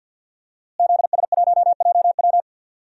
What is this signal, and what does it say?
Morse code, heard as a keyed tone, about 51 wpm, 700 Hz. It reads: ZS1JW
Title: DXCC-100 Call Signs - 5 Characters - Repeated 3x - 50wpm